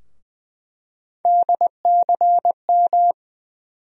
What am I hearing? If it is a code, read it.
Morse code, heard as a keyed tone, about 19 wpm, 700 Hz. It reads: DCM
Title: USA Call Sign Postfixes - 3 Letters 20wpm